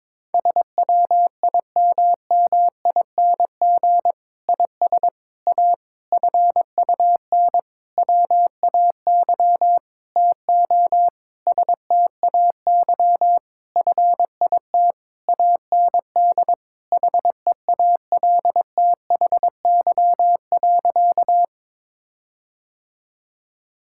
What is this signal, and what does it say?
Morse code, heard as a keyed tone, about 22 wpm, 700 Hz. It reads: SWIMMING IS A FUN WAY TO STAY FIT AND HEALTHY.
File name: Difficult Sentences with Hints - 22wpm